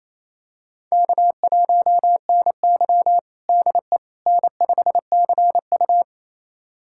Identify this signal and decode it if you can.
Morse code, heard as a keyed tone, about 28 wpm, 700 Hz. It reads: K1NY DE N5CU
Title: USA Callsign DE Callsign - 4 Characters - 28wpm